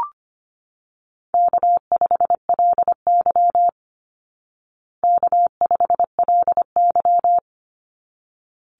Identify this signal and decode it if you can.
Morse code, heard as a keyed tone, about 25 wpm, 700 Hz. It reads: K5LY K5LY
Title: Easy Call Signs Repeated 2x - 25wpm